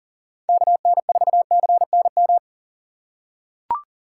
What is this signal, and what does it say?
Morse code, heard as a keyed tone, about 40 wpm, 700 Hz. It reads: KN4CNM
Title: USA Call Signs - 4x Character Spacing - 40wpm